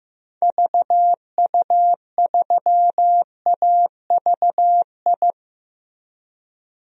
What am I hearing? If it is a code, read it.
Morse code, heard as a keyed tone, about 15 wpm, 700 Hz. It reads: VU3AVI
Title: DXCC-100 Call Signs - 6 Characters - Repeated 2x - 15wpm